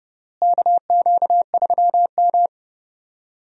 Morse code, heard as a keyed tone, about 31 wpm, 700 Hz. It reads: KQ3M
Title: USA Call Signs - 4 Characters - Repeated 3x - 30wpm